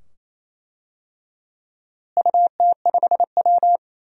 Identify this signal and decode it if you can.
Morse code, heard as a keyed tone, about 27 wpm, 700 Hz. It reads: UT5W
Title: DXCC-100 Call Signs - 4 Characters - Repeated 2x - 28wpm